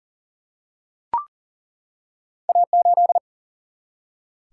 Morse code, heard as a keyed tone, about 40 wpm, 700 Hz. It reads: A8
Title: Single-Letter Single-Number 40wpm